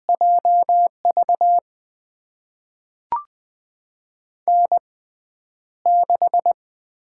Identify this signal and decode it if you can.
Morse code, heard as a keyed tone, about 20 wpm, spaced as for 10 wpm, 700 Hz. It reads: JV N6
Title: USA Call Signs - 5 Characters - 6x Character Spacing - 20wpm